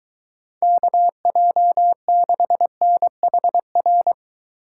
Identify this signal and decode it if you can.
Morse code, heard as a keyed tone, about 23 wpm, 700 Hz. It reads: KJ6NHR